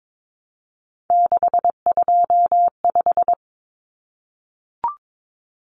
Morse code, heard as a keyed tone, about 22 wpm, 700 Hz. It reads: 625
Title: Numbers - 3-Digit 22wpm